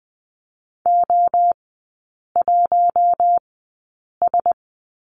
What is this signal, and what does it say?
Morse code, heard as a keyed tone, about 20 wpm, spaced as for 8 wpm, 690 Hz. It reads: O1S
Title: 093 Lesson 16e - Letter c - Three Characters 20wpm